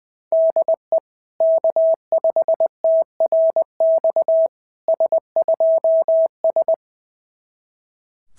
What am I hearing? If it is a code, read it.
Morse code, heard as a keyed tone, about 18 wpm, 650 Hz. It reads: DE K5TRX S2S
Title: SOTA QSO Practice 20wpm